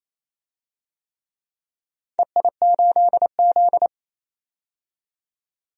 Morse code, heard as a keyed tone, about 28 wpm, 700 Hz. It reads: EI8Z
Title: DXCC-100 Call Signs - 4 Characters - 28wpm